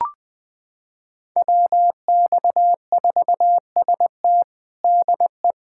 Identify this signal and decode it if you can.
Morse code, heard as a keyed tone, about 20 wpm, 700 Hz. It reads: WX4ST DE
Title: USA Callsign DE Callsign 20wpm